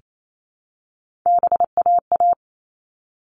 Morse code, heard as a keyed tone, about 28 wpm, 700 Hz. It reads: BAA
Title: All Cut Numbers - 3-Digit 28wpm